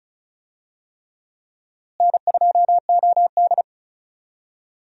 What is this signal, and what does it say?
Morse code, heard as a keyed tone, about 36 wpm, 700 Hz. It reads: N2OD
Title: USA Call Signs - 4 Characters - Repeated 3x - 35wpm